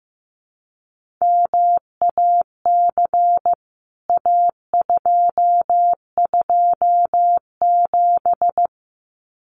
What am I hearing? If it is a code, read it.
Morse code, heard as a keyed tone, about 15 wpm, 700 Hz. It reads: MAC A227
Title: CWT Exchange 15wpm